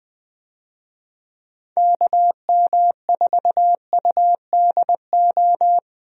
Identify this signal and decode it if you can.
Morse code, heard as a keyed tone, about 20 wpm, 700 Hz. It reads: KM4UDO
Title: USA Call Signs - 6 Characters - Repeated 2x - 20wpm